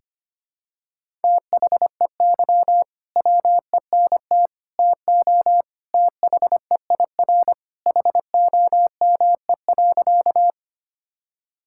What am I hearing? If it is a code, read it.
Morse code, heard as a keyed tone, about 25 wpm, 700 Hz. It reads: THEY WENT TO THEIR HOME.
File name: practice / 25wpm / Sentences with 2-Word Phrases - 25wpm